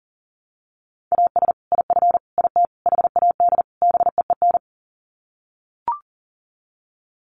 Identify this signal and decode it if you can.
Morse code, heard as a keyed tone, about 40 wpm, 700 Hz. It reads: AS IF IT HAD BEEN
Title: Sets of 5 Words Encore2 40wpm